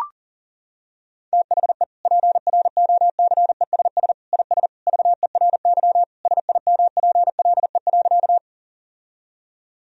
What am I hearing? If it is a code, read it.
Morse code, heard as a keyed tone, about 40 wpm, 700 Hz. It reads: THE PROCESS IS VERY SIMPLE.
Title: Sentences from Top 1000 Words - Review Part 3 - 40wpm